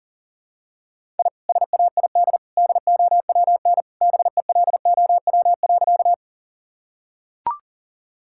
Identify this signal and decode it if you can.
Morse code, heard as a keyed tone, about 40 wpm, 700 Hz. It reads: I SAID DOWN BELOW.
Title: Sentences from Top 1000 Words - Review Part 5 - 40wpm